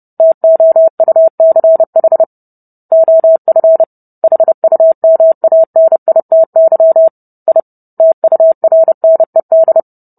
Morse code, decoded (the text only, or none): TOUCH OF HUMANITY I TURNED